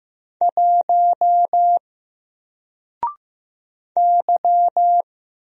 Morse code, heard as a keyed tone, about 15 wpm, 700 Hz. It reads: 1 Y
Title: Single Character (Letter:Number:Punctuation:Prosign) 15wpm